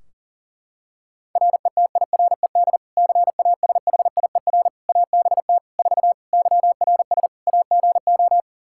Morse code, heard as a keyed tone, about 37 wpm, 700 Hz. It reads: RETIRED CASHIER ABT 4 YRS AGO
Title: Practice QSO - Exchange 3 - More Info - 40wpm